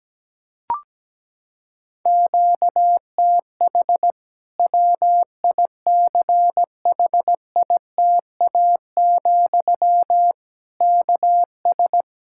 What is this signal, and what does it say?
Morse code, heard as a keyed tone, about 17 wpm, 700 Hz. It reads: QTH WICHITA, KS
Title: Practice QSO - Exchange 1 - Beginning - 17wpm